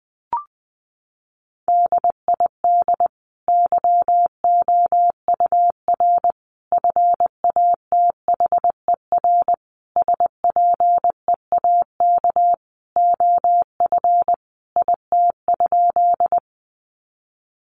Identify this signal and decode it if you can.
Morse code, heard as a keyed tone, about 20 wpm, 700 Hz. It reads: DID YOUR FATHER SPEAK OF IT?
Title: Sentences from Top 1100 Words 20wpm